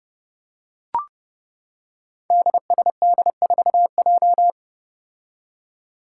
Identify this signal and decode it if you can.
Morse code, heard as a keyed tone, about 31 wpm, 700 Hz. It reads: DSD4J
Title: Five Challenging Letter-Number Combinations 30wpm